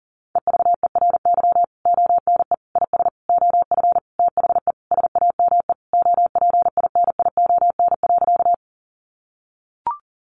Morse code, heard as a keyed tone, about 40 wpm, 700 Hz. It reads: EVERY ONE IS OF THE SAME OPINION.
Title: Sentences from Top 1200 Words 40wpm